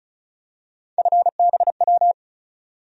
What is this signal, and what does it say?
Morse code, heard as a keyed tone, about 36 wpm, 700 Hz. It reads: FBW
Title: USA Call Sign Postfixes - 3 Letters 35wpm